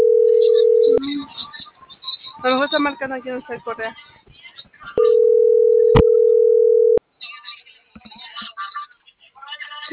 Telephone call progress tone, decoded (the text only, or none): ringback tone